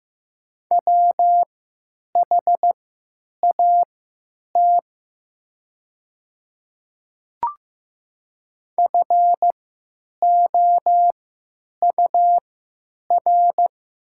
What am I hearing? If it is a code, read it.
Morse code, heard as a keyed tone, about 15 wpm, spaced as for 9 wpm, 700 Hz. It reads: WHAT FOUR
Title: 3x Character Spacing - Top 300 Words - 15wpm